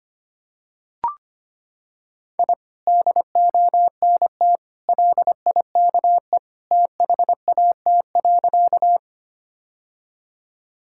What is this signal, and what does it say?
Morse code, heard as a keyed tone, about 25 wpm, 700 Hz. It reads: I DONT LIKE THAT.
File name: practice / 25wpm / Sentences with 2-Word Phrases - 25wpm